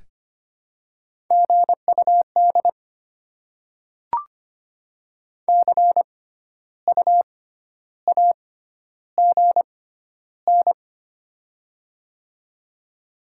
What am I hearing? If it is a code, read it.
Morse code, heard as a keyed tone, about 25 wpm, spaced as for 12 wpm, 700 Hz. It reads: GUD CU AGN
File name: Top 20 QSO Elements - 6x Character Spacing - 25wpm